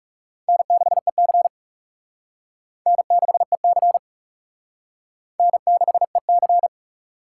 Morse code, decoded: N6EC N6EC N6EC

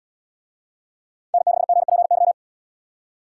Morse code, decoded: N6GPY

